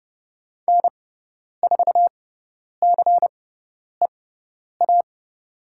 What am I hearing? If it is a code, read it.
Morse code, heard as a keyed tone, about 30 wpm, spaced as for 10 wpm, 690 Hz. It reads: N4CEA
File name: Lesson 17g - Letter u - Call Signs 30wpm